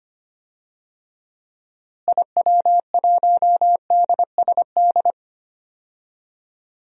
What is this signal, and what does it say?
Morse code, heard as a keyed tone, about 25 wpm, 700 Hz. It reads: IW1DSD
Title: DXCC-100 Call Signs - 6 Characters - Repeated 2x - 25wpm